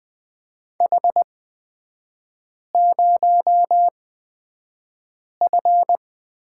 Morse code, heard as a keyed tone, about 20 wpm, spaced as for 5 wpm, 690 Hz. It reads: H0F
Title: 255 Lesson 41e - Period - Three Characters 20wpm